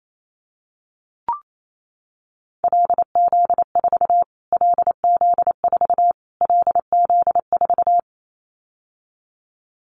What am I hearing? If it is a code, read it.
Morse code, heard as a keyed tone, about 28 wpm, 700 Hz. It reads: LZ4 LZ4 LZ4
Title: Callsign Prefixes for Top 25 DXCC - Repeated 3x - 28wpm